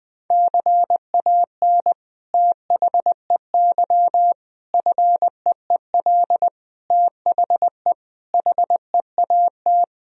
Morse code, heard as a keyed tone, about 21 wpm, 700 Hz. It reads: CAN THEY FEEL THE HEAT
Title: Sentences from Top 1000 Words - Review Part 3 - 20wpm